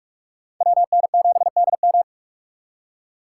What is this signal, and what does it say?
Morse code, heard as a keyed tone, about 41 wpm, 700 Hz. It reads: WN7DM